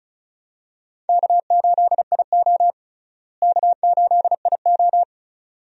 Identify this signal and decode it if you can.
Morse code, heard as a keyed tone, about 34 wpm, 700 Hz. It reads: K8IO K8IO
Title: USA Call Signs - 4 Characters - Repeated 2x - 35wpm